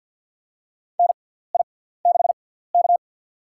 Morse code, heard as a keyed tone, about 51 wpm, spaced as for 17 wpm, 690 Hz. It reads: NI6K
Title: USA Call Signs - 4 Characters - 6x Character Spacing - 50wpm